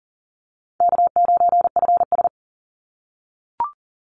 Morse code, heard as a keyed tone, about 40 wpm, 700 Hz. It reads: K9FS
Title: Easy Call Signs - 40wpm